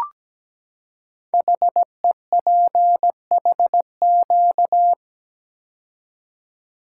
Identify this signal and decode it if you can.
Morse code, heard as a keyed tone, about 17 wpm, 700 Hz. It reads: HEPHQ
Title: Code Groups - 5 Letters - 17wpm